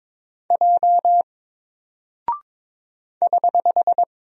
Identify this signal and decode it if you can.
Morse code, heard as a keyed tone, about 22 wpm, 700 Hz. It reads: J <HH>
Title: Single Character (Letter:Number:Punctuation:Prosign) Alt 22wpm